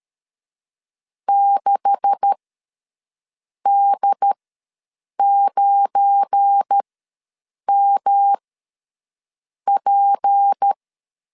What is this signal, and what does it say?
Morse code, heard as a keyed tone, about 13 wpm, spaced as for 6 wpm, 800 Hz. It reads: 6D9MP
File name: LEZIONI / 9 286 ws 12wpm 3351